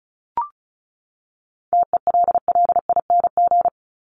Morse code, heard as a keyed tone, about 34 wpm, 700 Hz. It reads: TELLING
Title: ING Suffix - Top 100 Words with Base Words 35wpm